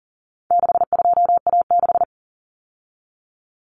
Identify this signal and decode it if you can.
Morse code, heard as a keyed tone, about 40 wpm, 700 Hz. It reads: 62A6
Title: Common Cut Numbers - 4-Digit 40wpm